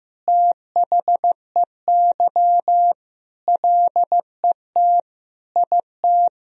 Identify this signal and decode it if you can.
Morse code, heard as a keyed tone, about 14 wpm, 700 Hz. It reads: THEY LET IT
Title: Sentences from Top 400 Words and Abbreviations 15wpm